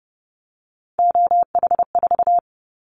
Morse code, heard as a keyed tone, about 30 wpm, 700 Hz. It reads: OH4
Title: Callsign Prefixes for Top 25 DXCC - 30wpm